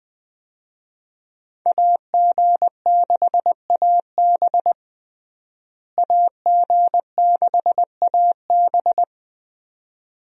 Morse code, heard as a keyed tone, about 20 wpm, 700 Hz. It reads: AG6AB AG6AB
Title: USA Call Signs - 5 Characters - Repeated 2x - 20wpm